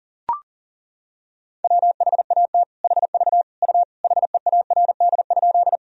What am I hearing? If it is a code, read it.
Morse code, heard as a keyed tone, about 39 wpm, 700 Hz. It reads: WHAT HV U HEARD?
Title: Sentences from Top 400 Words and Abbreviations 40wpm